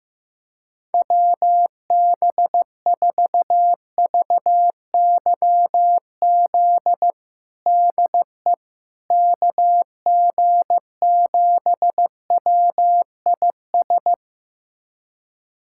Morse code, heard as a keyed tone, about 15 wpm, 700 Hz. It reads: WB4VYZ DE KG7WIS
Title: USA Callsign DE Callsign 15wpm